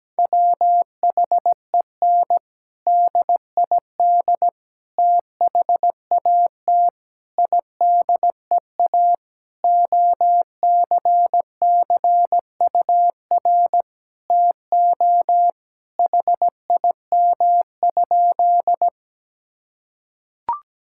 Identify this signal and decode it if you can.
Morse code, heard as a keyed tone, about 17 wpm, 700 Hz. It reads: WHEN DID THAT IDEA OCCUR TO HIM?
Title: Sentences from Top 1000 Words - Review Part 6 - 17wpm